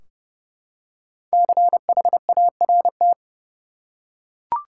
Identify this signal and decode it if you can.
Morse code, heard as a keyed tone, about 30 wpm, 700 Hz. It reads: CHART
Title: Five Letter Words - 30wpm